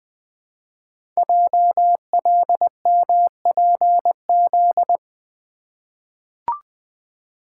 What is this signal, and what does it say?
Morse code, heard as a keyed tone, about 20 wpm, 700 Hz. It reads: JLMPZ
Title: Code Groups - 5 Letters - 20wpm